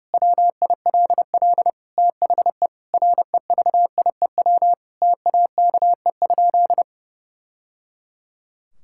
Morse code, heard as a keyed tone, about 30 wpm, 700 Hz. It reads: WILL THE REVIEW TAKE?
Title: Sentences from Top 1000 Words - Review Part 6 - 30wpm